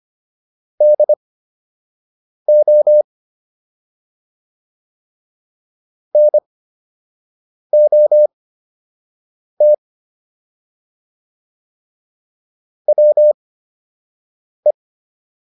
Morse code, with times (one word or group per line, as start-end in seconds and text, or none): 0.8-3.0 DO
6.1-9.7 NOT
12.9-14.7 WE